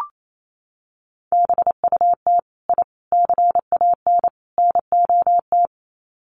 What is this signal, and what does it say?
Morse code, heard as a keyed tone, about 28 wpm, 700 Hz. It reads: BUT I CAN NOT
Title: Sets of 4 Words 28wpm